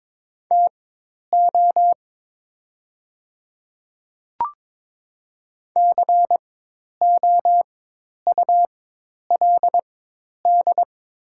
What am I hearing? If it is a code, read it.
Morse code, heard as a keyed tone, about 22 wpm, spaced as for 10 wpm, 700 Hz. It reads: TO COULD
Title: 4x Character Spacing - Top 100 Words - 22wpm